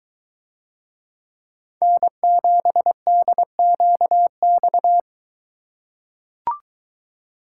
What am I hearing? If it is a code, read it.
Morse code, heard as a keyed tone, about 22 wpm, 700 Hz. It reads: N7DQX